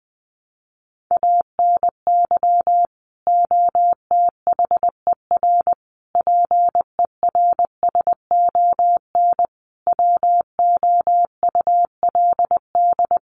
Morse code, heard as a keyed tone, about 19 wpm, 700 Hz. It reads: ANY OTHER PERSON WOULD
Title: Sentences from Top 1100 Words 20wpm